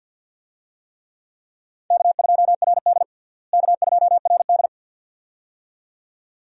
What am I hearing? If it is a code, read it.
Morse code, heard as a keyed tone, about 48 wpm, 700 Hz. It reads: K2RD K2RD
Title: USA Call Signs - 4 Characters - Repeated 2x - 50wpm